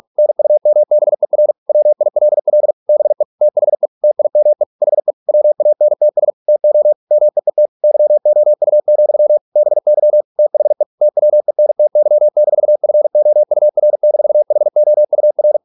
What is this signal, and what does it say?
Morse code, heard as a keyed tone, about 46 wpm, 600 Hz. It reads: NUMBER WILL BE THE TIME HE WANTS TO MEET YOU, BY THE TWENTY-FOUR-HOUR